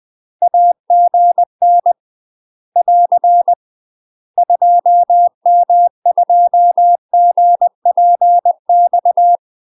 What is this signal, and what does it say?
Morse code, heard as a keyed tone, about 20 wpm, 700 Hz. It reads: AGN <AR> 2M2GPX